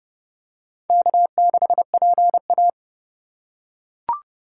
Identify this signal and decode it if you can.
Morse code, heard as a keyed tone, about 29 wpm, 700 Hz. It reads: K6PA